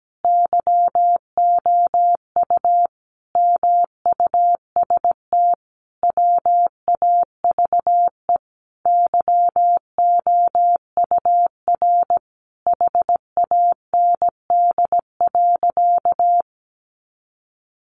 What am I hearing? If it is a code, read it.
Morse code, heard as a keyed tone, about 17 wpm, 700 Hz. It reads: YOU MUST WAVE YOUR HAND.
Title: Sentences from Top 1400 Words 17wpm